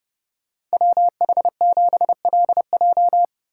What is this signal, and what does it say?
Morse code, heard as a keyed tone, about 33 wpm, 700 Hz. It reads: WH7LJ